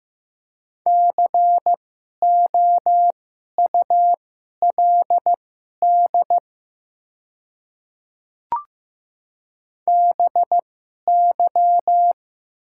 Morse code, heard as a keyed tone, about 15 wpm, spaced as for 11 wpm, 700 Hz. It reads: COULD BY
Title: 2x Character Spacing - Top 100 Words - 15wpm